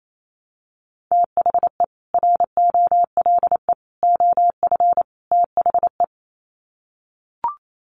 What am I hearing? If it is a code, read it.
Morse code, heard as a keyed tone, about 28 wpm, 700 Hz. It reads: THE ROLE OF THE
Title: Sets of 4 Words 28wpm